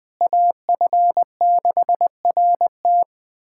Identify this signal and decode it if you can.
Morse code, heard as a keyed tone, about 21 wpm, 700 Hz. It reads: AF6RT
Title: USA Callsign DE Callsign - 5 Characters - 20wpm